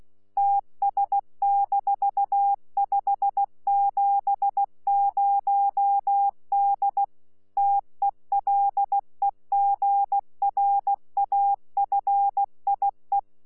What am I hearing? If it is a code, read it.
Morse code, heard as a keyed tone, about 16 wpm, 800 Hz. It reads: TS-570D TELEGRAFIE
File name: Z PI4VRZ / PI4VRZ 14-jun-2025